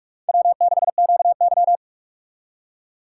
Morse code, heard as a keyed tone, about 57 wpm, 700 Hz. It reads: W6QY